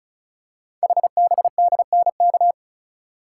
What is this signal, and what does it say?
Morse code, heard as a keyed tone, about 36 wpm, 690 Hz. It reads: HBDNK
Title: Code Groups - 5 Letters - 35wpm